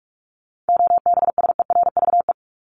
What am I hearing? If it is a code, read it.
Morse code, heard as a keyed tone, about 44 wpm, 700 Hz. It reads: OBSERVE
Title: ED Suffix - Top 100 Words with Base Words 45wpm